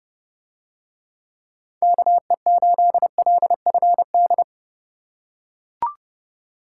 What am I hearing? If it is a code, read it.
Morse code, heard as a keyed tone, about 29 wpm, 700 Hz. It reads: KE8LFD